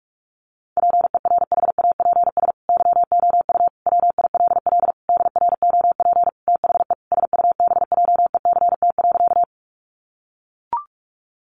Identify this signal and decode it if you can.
Morse code, heard as a keyed tone, about 44 wpm, 700 Hz. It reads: PERHAPS YOU WILL DROP THE SUBJECT.
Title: Sentences from Top 1400 Words 45wpm